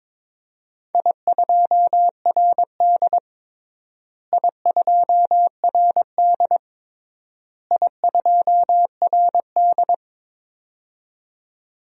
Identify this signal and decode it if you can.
Morse code, heard as a keyed tone, about 22 wpm, 700 Hz. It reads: I2RD I2RD I2RD
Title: DXCC-100 Call Signs - 4 Characters - Repeated 3x - 22wpm